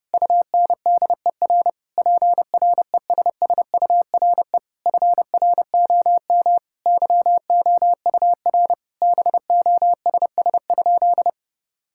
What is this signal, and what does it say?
Morse code, heard as a keyed tone, about 30 wpm, 700 Hz. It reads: UNDER PRESSURE FROM YOUR BOSS?
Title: Sentences from Top 1000 Words - Review Part 6 - 30wpm